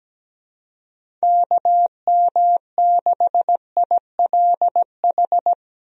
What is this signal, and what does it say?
Morse code, heard as a keyed tone, about 17 wpm, 700 Hz. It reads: KM6ILH
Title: USA Call Signs - 6 Characters - Repeated 2x - 17wpm